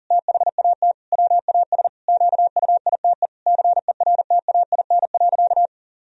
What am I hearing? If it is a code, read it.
Morse code, heard as a keyed tone, about 37 wpm, 700 Hz. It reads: THAT WAS QUITE CERTAIN.
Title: Sentences from Top 1000 Words - Review Part 2 - 40wpm